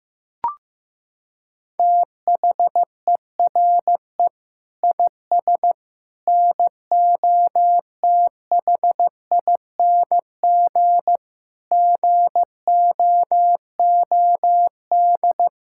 Morse code, heard as a keyed tone, about 15 wpm, 700 Hz. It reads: THERE IS NOTHING GOOD
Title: Sentences from Top 1000 Words - Review Part 5 - 15wpm